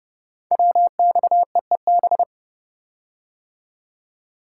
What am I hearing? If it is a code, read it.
Morse code, heard as a keyed tone, about 28 wpm, 700 Hz. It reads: WXEEB
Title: Code Groups - 5 Letters - 30wpm